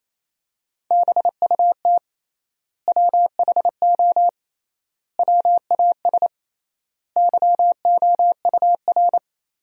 Morse code, heard as a keyed tone, about 28 wpm, 700 Hz. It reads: BUT WHO WAS YOUR
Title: 3x Word-Spacing - Sentences from Top 200 Words - 28wpm